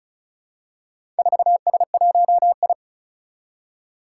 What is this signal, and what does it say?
Morse code, heard as a keyed tone, about 36 wpm, 700 Hz. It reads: 4S1I
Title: Four Challenging Letter-Number Combinations 35wpm